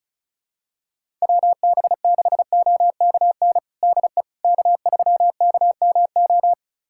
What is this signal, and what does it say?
Morse code, heard as a keyed tone, about 36 wpm, 700 Hz. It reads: WB6OKN DE K3KMO
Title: USA Callsign DE Callsign 35wpm